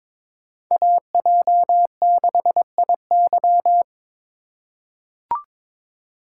Morse code, heard as a keyed tone, about 22 wpm, 700 Hz. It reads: AJ6IY
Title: USA Call Signs - 6x Character Spacing - 22wpm